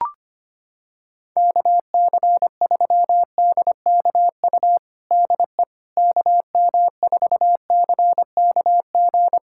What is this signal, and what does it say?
Morse code, heard as a keyed tone, about 25 wpm, 700 Hz. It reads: KC3DKU DE KM4CKG
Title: USA Callsign DE Callsign 25wpm